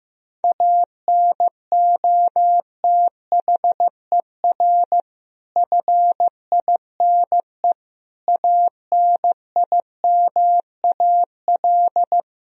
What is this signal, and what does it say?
Morse code, heard as a keyed tone, about 15 wpm, 700 Hz. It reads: ANOTHER FINE ANIMAL
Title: Sentences from Top 1000 Words - Review Part 1 - 15wpm